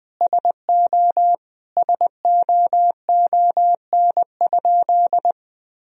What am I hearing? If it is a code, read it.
Morse code, heard as a keyed tone, about 20 wpm, 700 Hz. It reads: SO SOON?
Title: Sentences from Top 400 Words and Abbreviations 20wpm